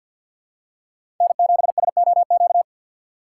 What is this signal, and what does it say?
Morse code, heard as a keyed tone, about 52 wpm, 700 Hz. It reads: N7SOQ
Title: USA Call Signs - 5 Characters - 2x Character Spacing - 50wpm